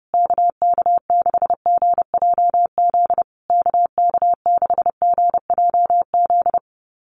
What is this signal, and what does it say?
Morse code, heard as a keyed tone, about 30 wpm, 700 Hz. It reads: KK6GJZ KK6GJZ
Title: USA Call Signs Repeated 3x 30wpm